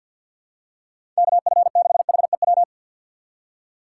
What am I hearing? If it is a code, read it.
Morse code, heard as a keyed tone, about 47 wpm, 700 Hz. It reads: KF6HEW
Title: USA Call Signs - 6 Characters - 6x Character Spacing - 50wpm